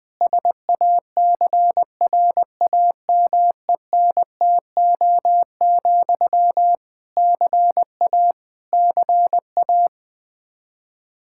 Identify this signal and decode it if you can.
Morse code, heard as a keyed tone, about 21 wpm, 700 Hz. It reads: SACRAMENTO, CA CA
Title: K1USN SST QSO Practice 20wpm